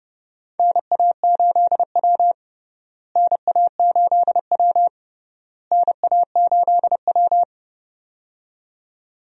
Morse code, heard as a keyed tone, about 32 wpm, 700 Hz. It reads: NA8W NA8W NA8W
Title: USA Call Signs - 4 Characters - Repeated 3x - 30wpm